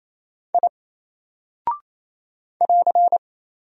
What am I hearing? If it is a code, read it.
Morse code, heard as a keyed tone, about 27 wpm, 700 Hz. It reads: I <AR>
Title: Single Character (Letter:Number:Punctuation:Prosign) 28wpm